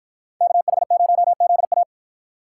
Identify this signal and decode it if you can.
Morse code, heard as a keyed tone, about 52 wpm, 700 Hz. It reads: KH0ZA